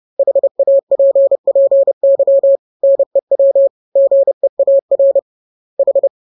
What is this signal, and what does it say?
Morse code, heard as a keyed tone, about 30 wpm, 550 Hz. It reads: HAPPY NEW GEAR H
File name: jokes / joke 17 30 0000